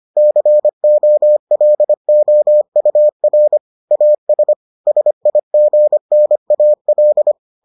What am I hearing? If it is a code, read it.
Morse code, heard as a keyed tone, about 25 wpm, 600 Hz. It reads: COLOUR AS SIGNAL